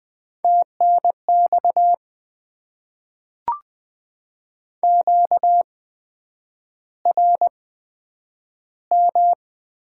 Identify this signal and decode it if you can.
Morse code, heard as a keyed tone, about 20 wpm, spaced as for 9 wpm, 700 Hz. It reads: TNX QRM
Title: Top 90 QSO Elements - 8x Character Spacing - 20wpm